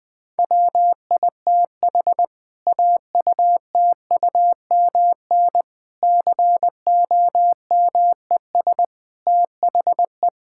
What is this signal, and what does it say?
Morse code, heard as a keyed tone, about 19 wpm, 700 Hz. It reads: WITH AUTUMN COMES THE
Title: Difficult Sentences with Hints - 20wpm